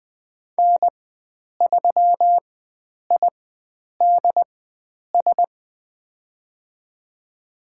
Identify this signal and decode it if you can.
Morse code, heard as a keyed tone, about 21 wpm, spaced as for 9 wpm, 700 Hz. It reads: N3IDS
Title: USA Call Signs - 4x Character Spacing - 20wpm